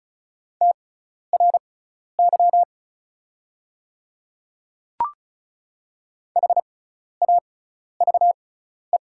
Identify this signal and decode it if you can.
Morse code, heard as a keyed tone, about 36 wpm, spaced as for 12 wpm, 690 Hz. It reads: TRY HAVE
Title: 6x Character Spacing - Top 300 Words - 35wpm